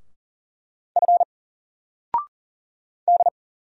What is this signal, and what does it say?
Morse code, heard as a keyed tone, about 39 wpm, 700 Hz. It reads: F D
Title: Single Letter-Number 40wpm